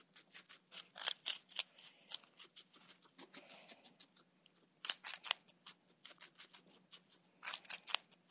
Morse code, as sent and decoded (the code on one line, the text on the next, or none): none